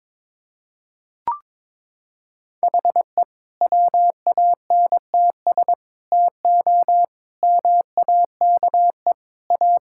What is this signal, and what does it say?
Morse code, heard as a keyed tone, about 21 wpm, 700 Hz. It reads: HE WANTS TO MAKE A
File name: practice / 22wpm / Sentences with 3-Word Phrases - 22wpm